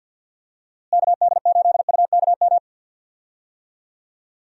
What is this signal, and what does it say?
Morse code, heard as a keyed tone, about 47 wpm, 700 Hz. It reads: KD8UKM
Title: USA Call Signs - 6 Characters - Repeated 2x - 50wpm